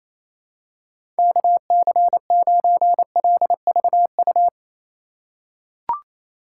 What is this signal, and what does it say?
Morse code, heard as a keyed tone, about 28 wpm, 700 Hz. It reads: KC9LVU
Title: USA Call Signs - 6 Characters - 3x Character Spacing - 28wpm